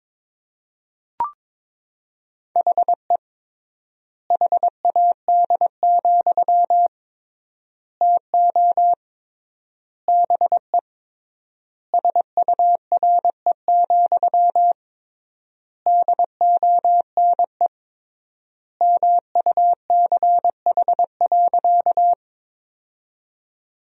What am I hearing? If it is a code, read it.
Morse code, heard as a keyed tone, about 22 wpm, 700 Hz. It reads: HE HAD, TO BE SURE, DONE MUCH.
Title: 3x Word-Spacing - Sentences from Top 500 Words - 22wpm